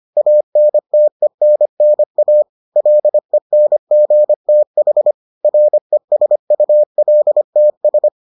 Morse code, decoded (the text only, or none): ANTENNA LENGTH RESULTS